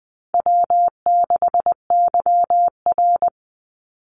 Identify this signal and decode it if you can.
Morse code, heard as a keyed tone, about 20 wpm, 700 Hz. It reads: W6YR
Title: Easy Call Signs Repeated 2x - 20wpm